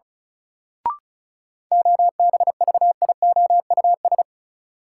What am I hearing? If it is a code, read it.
Morse code, heard as a keyed tone, about 36 wpm, 700 Hz. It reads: OBVIOUS
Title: Sentences from Top 1600 Words - New Word Drill - 35wpm